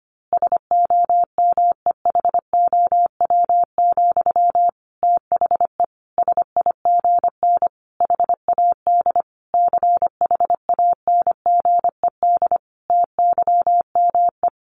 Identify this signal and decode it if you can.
Morse code, heard as a keyed tone, about 25 wpm, 700 Hz. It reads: SOMEHOW, THE SIGN HAD CHANGED TYME
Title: A Sound Of Thunder 25wpm